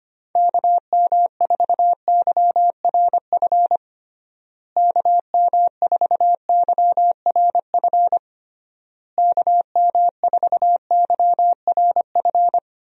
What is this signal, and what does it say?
Morse code, heard as a keyed tone, about 25 wpm, 700 Hz. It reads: KM4YRF KM4YRF KM4YRF
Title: USA Call Signs - 6 Characters - Repeated 3x - 25wpm